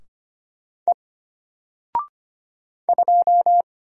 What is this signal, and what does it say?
Morse code, heard as a keyed tone, about 25 wpm, 700 Hz. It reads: E 2
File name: Single-Letter-Number-25wpm